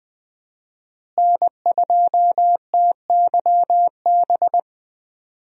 Morse code, heard as a keyed tone, about 20 wpm, 700 Hz. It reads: N2TYB
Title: USA Call Signs - 5 Characters - Repeated 3x - 20wpm